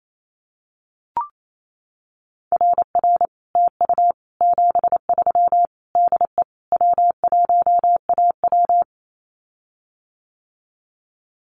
Morse code, heard as a keyed tone, about 28 wpm, 700 Hz. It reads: RR TU 73 DE W1AW
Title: Common QSO Phrases Encore 28wpm